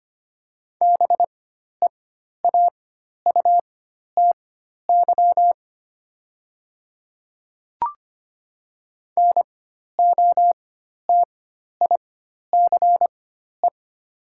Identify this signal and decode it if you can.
Morse code, heard as a keyed tone, about 25 wpm, spaced as for 12 wpm, 700 Hz. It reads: BEAUTY NOTICE
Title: 4x Character Spacing - Top 500 Words - 25wpm